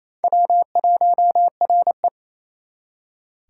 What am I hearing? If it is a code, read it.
Morse code, heard as a keyed tone, about 34 wpm, 700 Hz. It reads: W1RE
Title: Speed Racing - USA Call Signs 28wpm